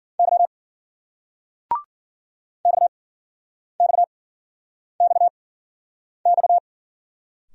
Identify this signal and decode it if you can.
Morse code, heard as a keyed tone, about 49 wpm, spaced as for 9 wpm, 700 Hz. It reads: Y XXXX
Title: Speed-Racing Single Letters 40wpm